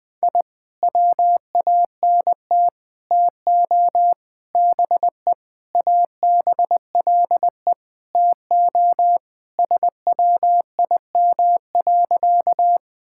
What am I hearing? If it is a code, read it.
Morse code, heard as a keyed tone, about 20 wpm, 700 Hz. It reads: I WANT TO BE ABLE TO SWIM.
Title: Sentences with 5-Word Phrases - Spoken Beginning - 20wpm